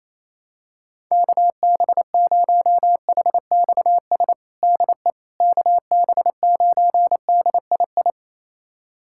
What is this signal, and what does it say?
Morse code, heard as a keyed tone, about 28 wpm, 700 Hz. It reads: KB0HXS DE KB9DII
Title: USA Callsign DE Callsign - 6 Characters - 28wpm